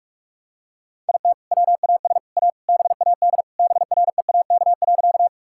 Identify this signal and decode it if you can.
Morse code, heard as a keyed tone, about 45 wpm, 700 Hz. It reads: IT WAS A BAD BREAK.
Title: Sentences from Top 1000 Words - Review Part 6 - 45wpm